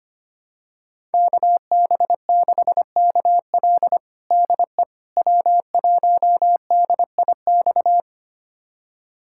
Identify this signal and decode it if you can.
Morse code, heard as a keyed tone, about 25 wpm, 700 Hz. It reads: KB6KL DE W1DIX
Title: USA Callsign DE Callsign 25wpm